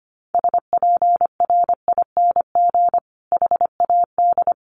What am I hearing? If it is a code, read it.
Morse code, heard as a keyed tone, about 25 wpm, 700 Hz. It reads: SPRING HAD
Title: Sentences from Top 1200 Words 25wpm